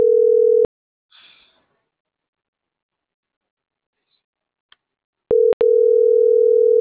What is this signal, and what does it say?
Ringback tone (North America), 440 + 480 Hz.